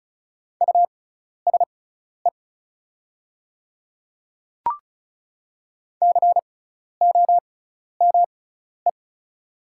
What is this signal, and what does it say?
Morse code, heard as a keyed tone, about 34 wpm, spaced as for 12 wpm, 700 Hz. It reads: USE COME
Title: 6x Character Spacing - Top 100 Words - 35wpm